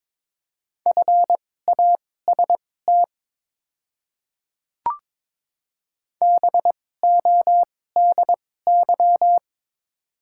Morse code, heard as a keyed tone, about 22 wpm, spaced as for 16 wpm, 690 Hz. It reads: FAST BODY
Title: 2x Character Spacing - Top 500 Words - 22wpm